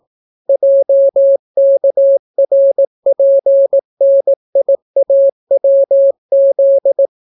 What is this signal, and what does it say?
Morse code, heard as a keyed tone, about 18 wpm, 550 Hz. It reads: JKRPNIAWZ